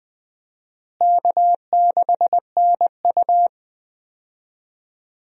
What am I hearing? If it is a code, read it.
Morse code, heard as a keyed tone, about 20 wpm, 700 Hz. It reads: K6NU
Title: USA Call Signs - 4 Characters - Repeated 2x - 20wpm